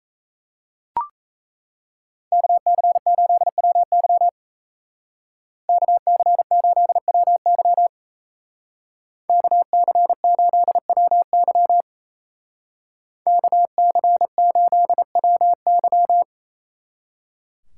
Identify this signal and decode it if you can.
Morse code, heard as a keyed tone, about 34 wpm, 700 Hz. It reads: KC8WY KC8WY KC8WY KC8WY
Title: Speed Racing - USA 2x2 Call Signs 28wpm